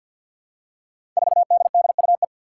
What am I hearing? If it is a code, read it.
Morse code, heard as a keyed tone, about 51 wpm, 700 Hz. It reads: 4DDUE
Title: Five Challenging Letter-Number Combinations 50wpm